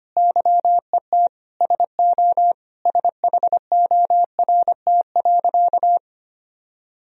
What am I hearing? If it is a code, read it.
Morse code, heard as a keyed tone, about 25 wpm, 700 Hz. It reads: YET SO SHORT.
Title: Sentences from Top 500 Words and Abbreviations 25wpm